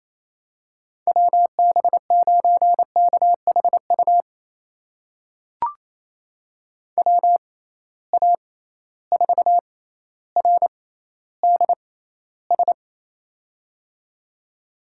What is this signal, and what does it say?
Morse code, heard as a keyed tone, about 27 wpm, spaced as for 14 wpm, 700 Hz. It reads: WB9KHU WA4RDS
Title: USA Call Signs - 6 Characters - 6x Character Spacing - 28wpm